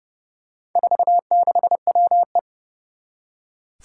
Morse code, heard as a keyed tone, about 32 wpm, 700 Hz. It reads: 46WE
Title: Four Challenging Letter-Number Combinations 30wpm